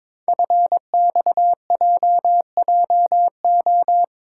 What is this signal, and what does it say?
Morse code, heard as a keyed tone, about 23 wpm, 700 Hz. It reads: FXJJO
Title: Code Groups - 5 Letters - 22wpm